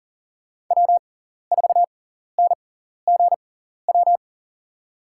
Morse code, heard as a keyed tone, about 40 wpm, spaced as for 14 wpm, 700 Hz. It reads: W4NGW
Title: USA Call Signs - 5 Characters - 6x Character Spacing - 40wpm